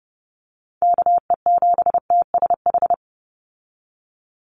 Morse code, heard as a keyed tone, about 30 wpm, 700 Hz. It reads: KE7TSH
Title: USA Call Signs 30wpm